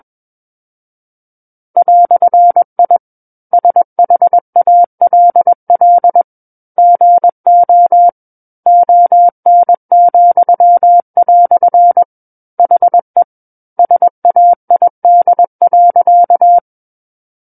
Morse code, heard as a keyed tone, about 21 wpm, 700 Hz. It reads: "I SHALL GO ON," HE SAID.
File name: Book One / 12-Chapter 12 - 21 WPM